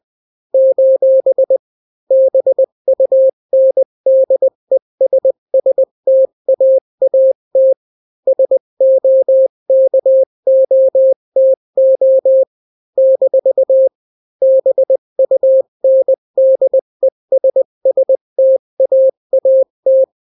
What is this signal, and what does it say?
Morse code, heard as a keyed tone, about 20 wpm, 530 Hz. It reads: : BUNDESSTAAT SOKOTO - BUNDESSTAAT
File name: daily / 18wpm